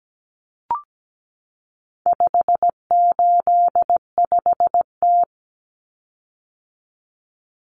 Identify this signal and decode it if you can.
Morse code, heard as a keyed tone, about 17 wpm, 700 Hz. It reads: 585T
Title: Common Cut Numbers - 4-Digit 17wpm